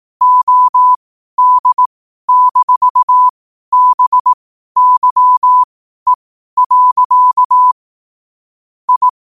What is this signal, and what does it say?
Morse code, heard as a keyed tone, about 18 wpm, spaced as for 12 wpm, 1000 Hz. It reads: OD-BYE. I
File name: qotdfiles / 12wpm